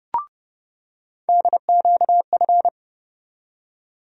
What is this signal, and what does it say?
Morse code, heard as a keyed tone, about 31 wpm, 700 Hz. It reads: DQF
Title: USA Call Sign Postfixes - 3 Letters 30wpm